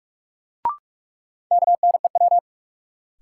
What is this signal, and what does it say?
Morse code, heard as a keyed tone, about 46 wpm, 700 Hz. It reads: KNEW
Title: Sentences from Top 400 Words - New Word Drill - 45wpm